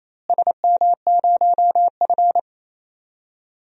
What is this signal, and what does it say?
Morse code, heard as a keyed tone, about 28 wpm, 700 Hz. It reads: SM0F
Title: DXCC-100 Call Signs - 4 Characters - Repeated 2x - 28wpm